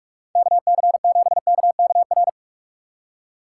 Morse code, heard as a keyed tone, about 54 wpm, 700 Hz. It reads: KC7KKR
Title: Speed Racing - USA 2x3 Call Signs 45wpm